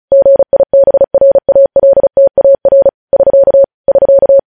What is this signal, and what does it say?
Morse code, heard as a keyed tone, about 36 wpm, 550 Hz. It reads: GIBRALTAR <SK> <SK>
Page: news headlines in morse code at 35 wpm